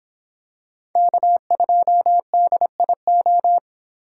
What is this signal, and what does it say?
Morse code, heard as a keyed tone, about 26 wpm, 700 Hz. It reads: K2DIO